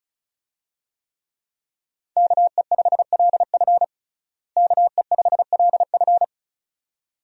Morse code, heard as a keyed tone, about 36 wpm, 700 Hz. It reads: KE5LF KE5LF
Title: USA Call Signs - 5 Characters - Repeated 2x - 35wpm